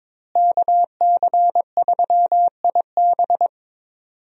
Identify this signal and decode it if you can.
Morse code, heard as a keyed tone, about 22 wpm, 700 Hz. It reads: KC3IB
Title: USA Call Signs - 5 Characters - Repeated 3x - 22wpm